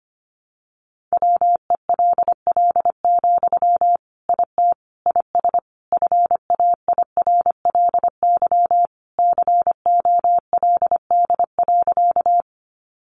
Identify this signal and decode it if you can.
Morse code, heard as a keyed tone, about 25 wpm, 700 Hz. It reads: WELL, IT IS FAIRLY COLD.
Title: Sentences from Top 1100 Words 25wpm